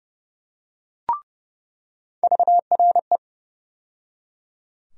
Morse code, heard as a keyed tone, about 30 wpm, 700 Hz. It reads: VRE
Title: USA Call Sign Postfixes - 3 Letters 30wpm